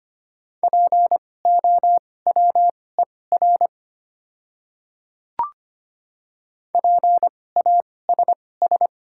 Morse code, heard as a keyed tone, about 25 wpm, spaced as for 18 wpm, 700 Hz. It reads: POWER PASS
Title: 2x Character Spacing - Top 500 Words - 25wpm